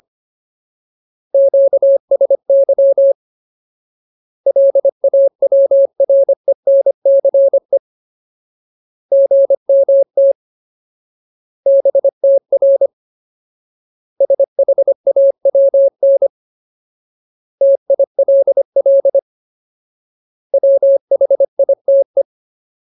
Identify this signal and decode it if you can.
Morse code, heard as a keyed tone, about 25 wpm, 550 Hz. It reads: QSY LAWRENCE GMT BTR SHAWN TILL WHITE
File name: Ultimate Word List / 25WPM Ultimate